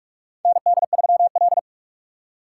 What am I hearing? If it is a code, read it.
Morse code, heard as a keyed tone, about 46 wpm, 700 Hz. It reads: ND3L
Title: K1USN SST QSO Practice - 45wpm